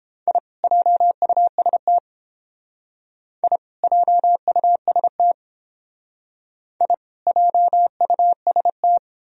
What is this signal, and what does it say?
Morse code, heard as a keyed tone, about 30 wpm, 700 Hz. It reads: I JUST I JUST I JUST